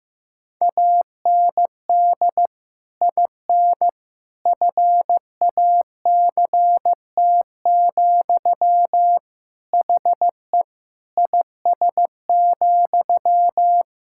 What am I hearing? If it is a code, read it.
Morse code, heard as a keyed tone, about 15 wpm, 700 Hz. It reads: AND IN FACT, HE IS,
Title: Sentences from Top 1600 Words 15wpm